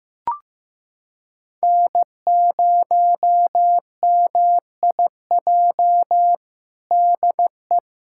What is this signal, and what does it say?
Morse code, heard as a keyed tone, about 15 wpm, 700 Hz. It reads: N0MIJ DE
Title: USA Callsign DE Callsign - 5 Characters - 15wpm